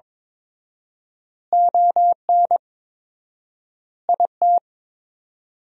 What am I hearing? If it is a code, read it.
Morse code, heard as a keyed tone, about 22 wpm, 700 Hz. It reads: ON IT
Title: 4x Word-Spacing - Sentences from Top 400 Words - 22wpm